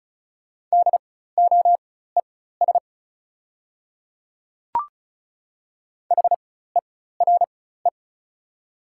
Morse code, heard as a keyed tone, about 36 wpm, spaced as for 16 wpm, 690 Hz. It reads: DOES HERE
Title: 4x Character Spacing - Top 500 Words - 35wpm